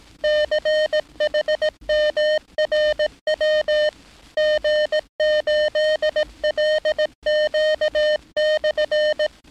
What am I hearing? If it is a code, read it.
Morse code, heard as a keyed tone, about 18 wpm, 610 Hz. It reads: CHMRW G8LQ/
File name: training / NAVPERS 11049to11060 / NAVPERS 11059 RA RB